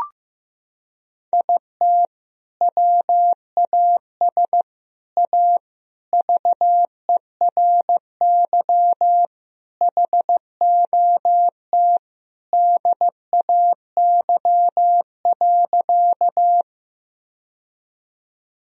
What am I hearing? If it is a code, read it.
Morse code, heard as a keyed tone, about 15 wpm, 700 Hz. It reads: IT WAS A VERY HOT DAY.
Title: Sentences from Top 1000 Words - Review Part 3 - 15wpm